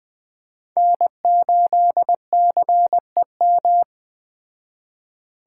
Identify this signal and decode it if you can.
Morse code, heard as a keyed tone, about 21 wpm, 700 Hz. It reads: N8CEM
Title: Speed Racing - USA 1x3 Call Signs 17wpm